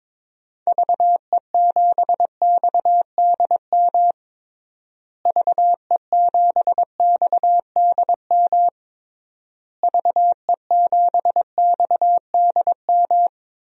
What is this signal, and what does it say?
Morse code, heard as a keyed tone, about 22 wpm, 700 Hz. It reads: VE7XDM VE7XDM VE7XDM
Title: DXCC-100 Call Signs - 6 Characters - Repeated 3x - 22wpm